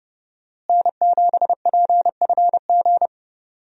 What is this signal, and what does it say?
Morse code, heard as a keyed tone, about 30 wpm, 700 Hz. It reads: N7PFG
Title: USA Call Signs - 5 Characters - Repeated 2x - 30wpm